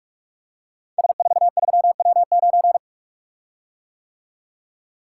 Morse code, heard as a keyed tone, about 44 wpm, 700 Hz. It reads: S43W9
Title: Code Groups - 5 Letter:Numbers - 45wpm